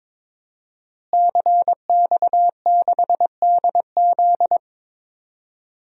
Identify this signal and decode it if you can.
Morse code, heard as a keyed tone, about 22 wpm, 700 Hz. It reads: CX6DZ
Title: DXCC-100 Call Signs - 5 Characters - Repeated 3x - 22wpm